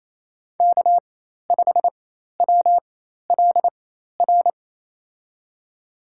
Morse code, heard as a keyed tone, about 28 wpm, spaced as for 13 wpm, 700 Hz. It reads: K5WLR
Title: USA Call Signs - 5 Characters - 4x Character Spacing - 28wpm